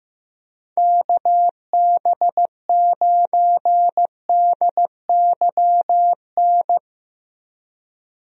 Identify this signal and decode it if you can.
Morse code, heard as a keyed tone, about 15 wpm, 700 Hz. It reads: KB9DYN
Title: USA Call Signs - 6 Characters - 15wpm